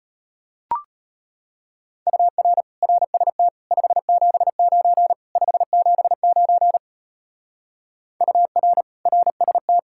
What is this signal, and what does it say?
Morse code, heard as a keyed tone, about 32 wpm, 700 Hz. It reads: UR RST 579 579 UR RST